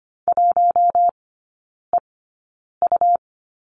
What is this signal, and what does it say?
Morse code, heard as a keyed tone, about 25 wpm, spaced as for 9 wpm, 700 Hz. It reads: 1EU
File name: Lesson 17e - Letter u - Three Characters 25wpm